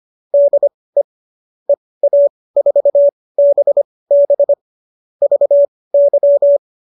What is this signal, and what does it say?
Morse code, heard as a keyed tone, about 25 wpm, spaced as for 18 wpm, 570 Hz. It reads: DE EA4BB VY